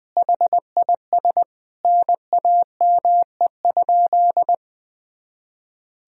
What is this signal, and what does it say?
Morse code, heard as a keyed tone, about 20 wpm, 700 Hz. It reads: HIS NAME?
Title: Sentences from Top 200 Words and Abbreviations 20wpm